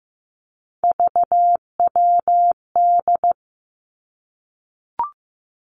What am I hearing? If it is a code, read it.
Morse code, heard as a keyed tone, about 15 wpm, 700 Hz. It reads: VWD
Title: Three Challenging Letter-Number Combinations 15wpm